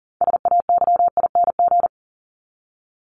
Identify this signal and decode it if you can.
Morse code, heard as a keyed tone, about 40 wpm, 700 Hz. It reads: SAYING
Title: ING Suffix - Top 100 Words with Base Words 40wpm